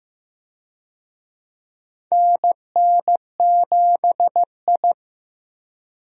Morse code, heard as a keyed tone, about 15 wpm, 700 Hz. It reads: NN7I
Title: USA Call Signs - 4 Characters - Repeated 2x - 15wpm